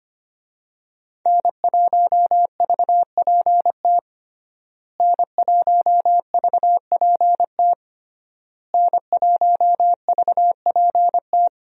Morse code, heard as a keyed tone, about 25 wpm, 700 Hz. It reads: N1VPT N1VPT N1VPT
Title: USA Call Signs - 5 Characters - Repeated 3x - 25wpm